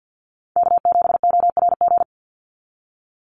Morse code, heard as a keyed tone, about 48 wpm, 700 Hz. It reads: K7ORG
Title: NA SOTA Chaser Callsigns 50wpm